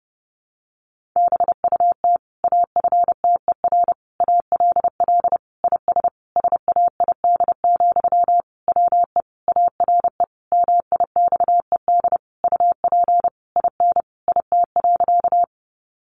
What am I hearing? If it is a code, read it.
Morse code, heard as a keyed tone, about 30 wpm, 700 Hz. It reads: BUT AFTER ALL IS SAID, WE ARE MIXED UP IN IT.
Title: Sentences from Top 1500 Words 30wpm